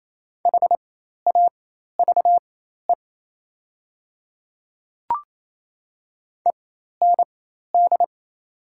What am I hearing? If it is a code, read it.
Morse code, heard as a keyed tone, about 28 wpm, spaced as for 13 wpm, 700 Hz. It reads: HAVE END
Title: 4x Character Spacing - Top 200 Words - 28wpm